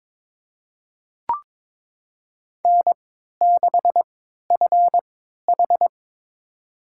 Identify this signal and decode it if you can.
Morse code, heard as a keyed tone, about 23 wpm, spaced as for 13 wpm, 700 Hz. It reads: N6FH
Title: USA Call Signs - 4 Characters - 3x Character Spacing - 22wpm